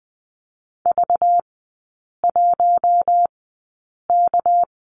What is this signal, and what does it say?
Morse code, heard as a keyed tone, about 20 wpm, spaced as for 8 wpm, 690 Hz. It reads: V1K
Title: 205 Lesson 33e - Letter j - Three Characters 20wpm